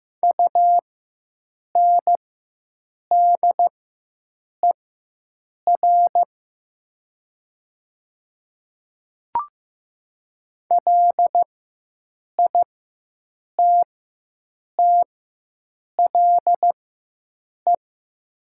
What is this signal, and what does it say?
Morse code, heard as a keyed tone, about 15 wpm, spaced as for 7 wpm, 700 Hz. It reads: UNDER LITTLE
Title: 4x Character Spacing - Top 300 Words - 15wpm